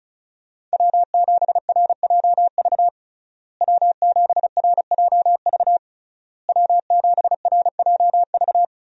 Morse code, heard as a keyed tone, about 36 wpm, 700 Hz. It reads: W7RJV W7RJV W7RJV
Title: USA Call Signs - 5 Characters - Repeated 3x - 35wpm